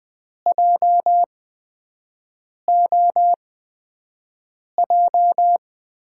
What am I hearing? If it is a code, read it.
Morse code, heard as a keyed tone, about 20 wpm, spaced as for 5 wpm, 700 Hz. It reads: JOJ
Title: USA Call Signs - 6 Characters - 8x Character Spacing - 20wpm